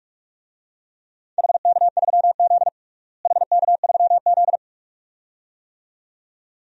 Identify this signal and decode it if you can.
Morse code, heard as a keyed tone, about 46 wpm, 700 Hz. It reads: HK3Z HK3Z
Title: DXCC-100 Call Signs - 4 Characters - Repeated 2x - 45wpm